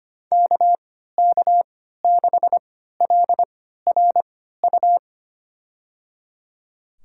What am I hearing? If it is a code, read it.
Morse code, heard as a keyed tone, about 25 wpm, spaced as for 14 wpm, 700 Hz. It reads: KK6LRU
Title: USA Call Signs - 6 Characters - 3x Character Spacing - 25wpm